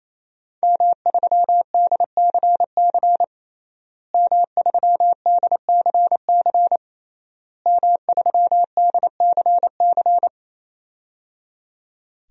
Morse code, heard as a keyed tone, about 27 wpm, 700 Hz. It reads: M3DCC M3DCC M3DCC
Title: DXCC-100 Call Signs - 5 Characters - Repeated 3x - 28wpm